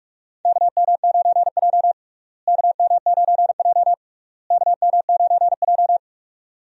Morse code, heard as a keyed tone, about 46 wpm, 700 Hz. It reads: KM9J KM9J KM9J
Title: USA Call Signs - 4 Characters - Repeated 3x - 45wpm